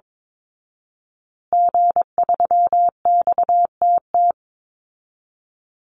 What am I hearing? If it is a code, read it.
Morse code, heard as a keyed tone, about 22 wpm, 700 Hz. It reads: G3XTT
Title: Call Signs Repeated 2x - 22wpm